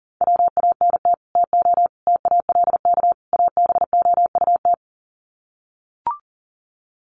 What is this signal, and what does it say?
Morse code, heard as a keyed tone, about 40 wpm, 700 Hz. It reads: WANT TO TALK ABOUT
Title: Sets of 4 Words 40wpm